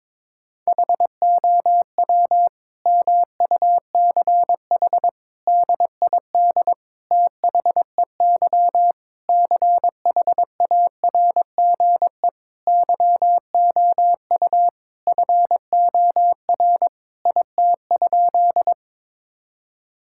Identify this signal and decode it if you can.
Morse code, heard as a keyed tone, about 22 wpm, 700 Hz. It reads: HOW MUCH DID THEY CHARGE YOU FOR IT?
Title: Sentences from Top 1000 Words - Review Part 6 - 22wpm